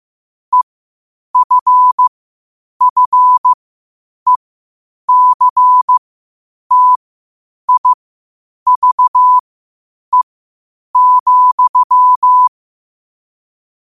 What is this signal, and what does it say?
Morse code, heard as a keyed tone, about 15 wpm, spaced as for 8 wpm, 1000 Hz. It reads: EFFECTIVE,
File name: qotdfiles / 8wpm